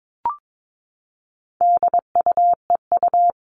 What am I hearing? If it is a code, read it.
Morse code, heard as a keyed tone, about 22 wpm, 700 Hz. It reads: DUEU
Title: All Cut Numbers - 4-Digit 22wpm